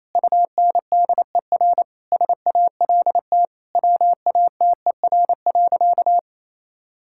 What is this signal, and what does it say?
Morse code, heard as a keyed tone, about 28 wpm, 700 Hz. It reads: UNDER SALT WATER.
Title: Sentences from Top 1000 Words - Review Part 5 - 28wpm